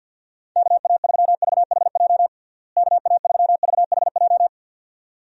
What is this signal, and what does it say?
Morse code, heard as a keyed tone, about 49 wpm, 700 Hz. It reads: KA3VHJ KA3VHJ
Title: USA Call Signs - 6 Characters - Repeated 2x - 50wpm